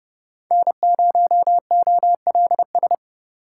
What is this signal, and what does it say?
Morse code, heard as a keyed tone, about 30 wpm, 700 Hz. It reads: N0OLS
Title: USA Call Signs - 5 Characters - 8x Character Spacing - 30wpm